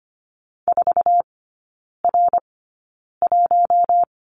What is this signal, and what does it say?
Morse code, heard as a keyed tone, about 25 wpm, spaced as for 9 wpm, 690 Hz. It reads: 4R1
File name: Lesson 15e - Number 5 - Three Characters 25wpm